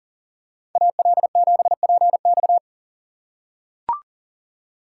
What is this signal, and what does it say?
Morse code, heard as a keyed tone, about 40 wpm, 700 Hz. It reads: AL7PX
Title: USA Call Signs - 5 Characters - 2x Character Spacing - 40wpm